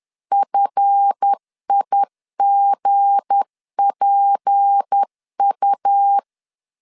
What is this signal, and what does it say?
Morse code, heard as a keyed tone, about 10 wpm, 800 Hz. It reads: FIGPU
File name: LEZIONI / 7 GF9Z / 7 GF 10wpm 2514